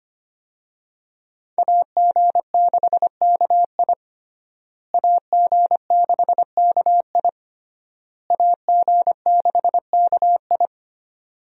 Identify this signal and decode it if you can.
Morse code, heard as a keyed tone, about 25 wpm, 700 Hz. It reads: AG6KI AG6KI AG6KI
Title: USA Call Signs - 5 Characters - Repeated 3x - 25wpm